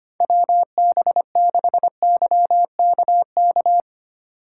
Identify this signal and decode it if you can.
Morse code, heard as a keyed tone, about 25 wpm, 700 Hz. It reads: WB6YKK
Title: USA Call Signs - 6 Characters - 4x Character Spacing - 25wpm